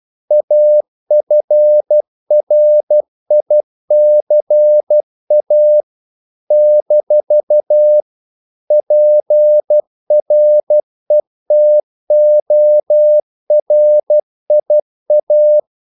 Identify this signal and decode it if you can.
Morse code, heard as a keyed tone, about 12 wpm, 600 Hz. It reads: AFRICA - PRETORIA
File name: CW / Practice / NationalCapitals12